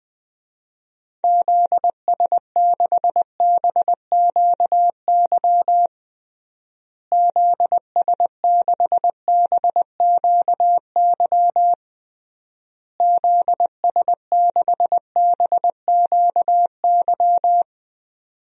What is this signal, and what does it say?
Morse code, heard as a keyed tone, about 20 wpm, 700 Hz. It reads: ZS6BQY ZS6BQY ZS6BQY
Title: DXCC-100 Call Signs - 6 Characters - Repeated 3x - 20wpm